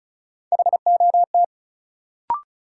Morse code, heard as a keyed tone, about 36 wpm, 700 Hz. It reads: HOT
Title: Sentences from Top 500 Words - New Word Drill - 35wpm